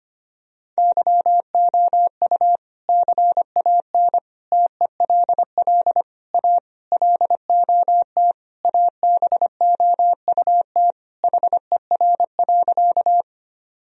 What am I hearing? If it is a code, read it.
Morse code, heard as a keyed tone, about 25 wpm, 700 Hz. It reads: YOU CAN TELL A LOT ABOUT HER.
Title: Sentences from Top 1000 Words - Review Part 3 - 25wpm